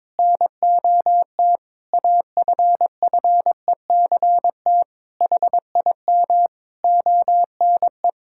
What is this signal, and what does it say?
Morse code, heard as a keyed tone, about 21 wpm, 700 Hz. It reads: NOT AFFECT HIM ONE
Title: Sentences from Top 1000 Words - Review Part 6 - 22wpm